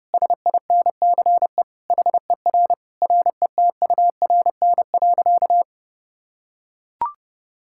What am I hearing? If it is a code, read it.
Morse code, heard as a keyed tone, about 30 wpm, 700 Hz. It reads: SINCE HER RETURN.
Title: Sentences from Top 1000 Words - Review Part 6 - 30wpm